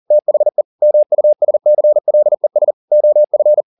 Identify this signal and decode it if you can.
Morse code, heard as a keyed tone, about 40 wpm, 600 Hz. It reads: THE MUSCLES OF